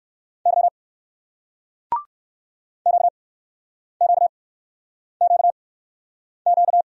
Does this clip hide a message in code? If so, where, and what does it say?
Yes: Morse code from 0.5 to 6.8 s, heard as a keyed tone, about 61 wpm, spaced as for 9 wpm, 700 Hz. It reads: Y QQQQ